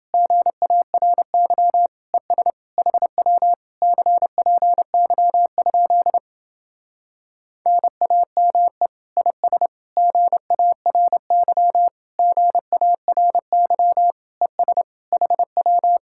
Morse code, heard as a keyed tone, about 29 wpm, 700 Hz. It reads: GARY ES HW CPY? NAME IS GARY GARY ES HW